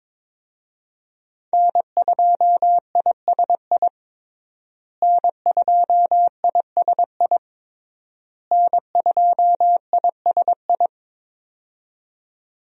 Morse code, heard as a keyed tone, about 21 wpm, 700 Hz. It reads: N2ISI N2ISI N2ISI
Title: USA Call Signs - 5 Characters - Repeated 3x - 22wpm